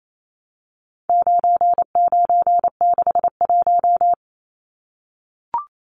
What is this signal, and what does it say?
Morse code, heard as a keyed tone, about 28 wpm, 700 Hz. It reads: 9961
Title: Numbers - 4-Digit 28wpm